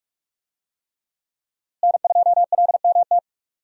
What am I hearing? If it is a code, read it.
Morse code, heard as a keyed tone, about 46 wpm, 700 Hz. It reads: N2LMT
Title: USA Call Signs - 5 Characters - Repeated 3x - 45wpm